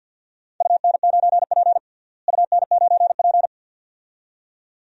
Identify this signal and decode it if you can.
Morse code, heard as a keyed tone, about 53 wpm, 700 Hz. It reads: UN9P UN9P
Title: DXCC-100 Call Signs - 4 Characters - Repeated 2x - 50wpm